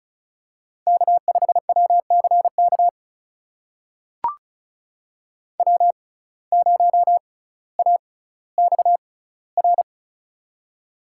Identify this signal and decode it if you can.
Morse code, heard as a keyed tone, about 36 wpm, spaced as for 18 wpm, 700 Hz. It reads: K5WCK W0AXR
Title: USA Call Signs - 6x Character Spacing - 35wpm